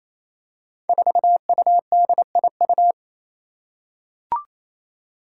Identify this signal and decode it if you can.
Morse code, heard as a keyed tone, about 27 wpm, 700 Hz. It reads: 4UDIU
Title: Five Challenging Letter-Number Combinations 28wpm